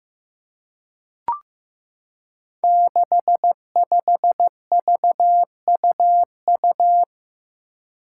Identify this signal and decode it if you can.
Morse code, heard as a keyed tone, about 15 wpm, 700 Hz. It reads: 65VUU
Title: Five Challenging Letter-Number Combinations 15wpm